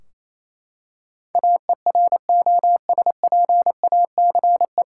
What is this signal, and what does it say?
Morse code, heard as a keyed tone, about 28 wpm, 700 Hz. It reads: AEROSPACE
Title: Practice QSO - Exchange 3 - More Info - 28wpm